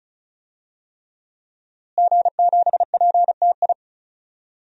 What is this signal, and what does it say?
Morse code, heard as a keyed tone, about 34 wpm, 700 Hz. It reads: G7PTI
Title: DXCC-100 Call Signs - 5 Characters - Repeated 3x - 35wpm